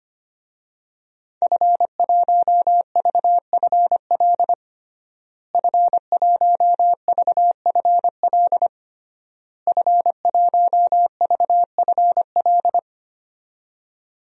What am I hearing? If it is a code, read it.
Morse code, heard as a keyed tone, about 25 wpm, 700 Hz. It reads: F1VFL F1VFL F1VFL
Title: DXCC-100 Call Signs - 5 Characters - Repeated 3x - 25wpm